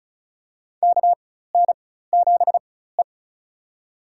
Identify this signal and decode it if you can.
Morse code, heard as a keyed tone, about 34 wpm, spaced as for 16 wpm, 700 Hz. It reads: KN7E
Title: USA Call Signs - 4 Characters - 4x Character Spacing - 35wpm